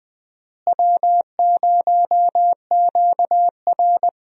Morse code, heard as a keyed tone, about 19 wpm, 700 Hz. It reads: W0QR
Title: USA Call Signs - 4 Characters - Repeated 2x - 20wpm